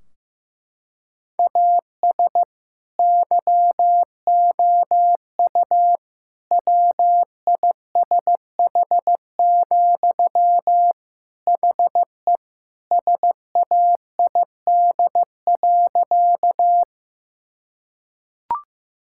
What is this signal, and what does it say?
Morse code, heard as a keyed tone, about 16 wpm, 700 Hz. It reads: AS YOU WISH, HE SAID.
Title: Sentences from Top 1600 Words 15wpm